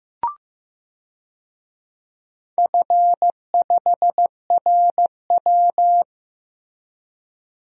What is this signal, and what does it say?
Morse code, heard as a keyed tone, about 16 wpm, 700 Hz. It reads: F5RW
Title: DXCC-100 Call Signs - 4 Characters - Repeated 3x - 15wpm